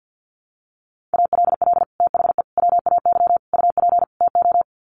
Morse code, heard as a keyed tone, about 50 wpm, 700 Hz. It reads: ALL THE WAY UP TO
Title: Sets of 5 Words Encore 50wpm